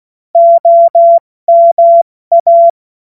Morse code, heard as a keyed tone, about 16 wpm, 680 Hz. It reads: OMA
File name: Morsekurs / Witze / Witz 16. Mai 2023 70 F80